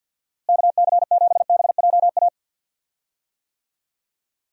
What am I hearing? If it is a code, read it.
Morse code, heard as a keyed tone, about 50 wpm, 700 Hz. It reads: KC7BJA
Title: USA Call Signs - 6 Characters - Repeated 3x - 50wpm